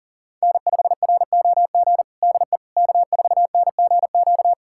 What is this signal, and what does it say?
Morse code, heard as a keyed tone, about 37 wpm, 700 Hz. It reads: N5ROG DE K4NGQ
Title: USA Callsign DE Callsign - 5 Characters - 40wpm